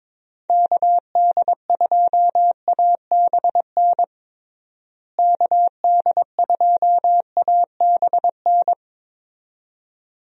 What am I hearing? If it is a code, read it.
Morse code, heard as a keyed tone, about 22 wpm, 700 Hz. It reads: KD2ABN KD2ABN
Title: USA Call Signs - 6 Characters - Repeated 2x - 22wpm